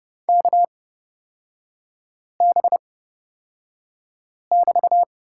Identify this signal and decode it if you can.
Morse code, heard as a keyed tone, about 30 wpm, spaced as for 5 wpm, 700 Hz. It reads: KB<BT>
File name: Lesson 36e - BT - Three Characters 30wpm